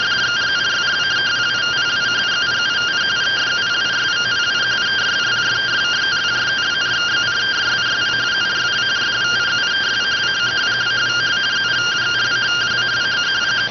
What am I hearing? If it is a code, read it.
Two-tone teleprinter signal, about 170 Hz shift.